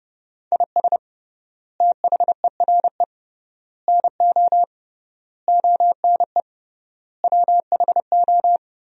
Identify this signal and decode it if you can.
Morse code, heard as a keyed tone, about 31 wpm, 700 Hz. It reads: IS THERE NO ONE WHO
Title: 3x Word-Spacing - Sentences from Top 200 Words - 30wpm